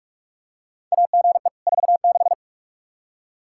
Morse code, heard as a keyed tone, about 42 wpm, 700 Hz. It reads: AGE 46
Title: Practice QSO - Exchange 3 - More Info - 45wpm